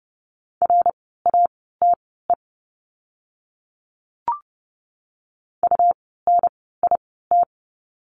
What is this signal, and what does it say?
Morse code, heard as a keyed tone, about 30 wpm, spaced as for 17 wpm, 700 Hz. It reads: RATE UNIT
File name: Lesson 17f - Letter u - Words 30wpm